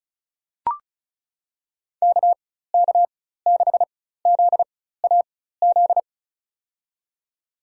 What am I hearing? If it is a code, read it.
Morse code, heard as a keyed tone, about 36 wpm, spaced as for 16 wpm, 700 Hz. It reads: KK6ZAZ
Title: USA Call Signs - 4x Character Spacing - 35wpm